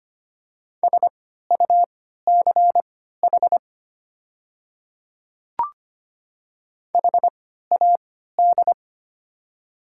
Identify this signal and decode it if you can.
Morse code, heard as a keyed tone, about 25 wpm, spaced as for 14 wpm, 700 Hz. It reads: SUCH HAD
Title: 3x Character Spacing - Top 200 Words - 25wpm